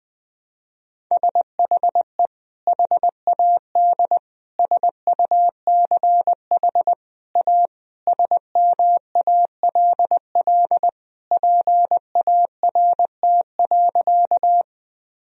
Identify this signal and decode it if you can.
Morse code, heard as a keyed tone, about 21 wpm, 700 Hz. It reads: SHE HAD SUCH A SMALL PART.
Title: Sentences from Top 1000 Words - Review Part 1 - 20wpm